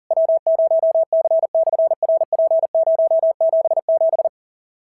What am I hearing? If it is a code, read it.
Morse code, heard as a keyed tone, about 40 wpm, 650 Hz. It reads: W0C/RP077
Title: SOTA QSO Practice 40wpm